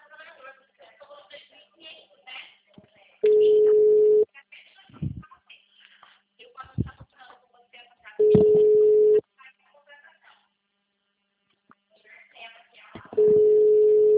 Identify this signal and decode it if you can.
Ringback tone (Europe, 425 Hz), with speech.
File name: out- -2025- - - .13